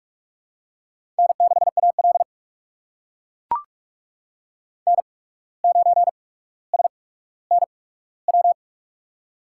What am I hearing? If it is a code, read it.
Morse code, heard as a keyed tone, about 43 wpm, spaced as for 18 wpm, 700 Hz. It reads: N6AL N9SNW
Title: USA Call Signs - 8x Character Spacing - 45wpm